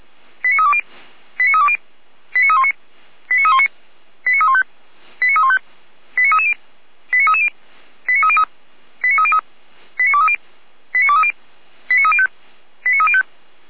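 Five-tone selective call (ZVEI), heard as quick runs of five tones, tones of about 70 ms. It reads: 89320 89320 89319 89319 89315 89315 89330 89330 89303 89303 89230 89230 89305 89305
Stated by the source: ZVEI five-tone selective call received by SDR.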